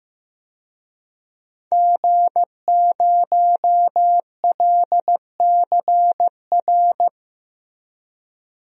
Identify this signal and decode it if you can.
Morse code, heard as a keyed tone, about 15 wpm, 700 Hz. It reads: G0LCR
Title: DXCC-100 Call Signs - 5 Characters - Repeated 2x - 15wpm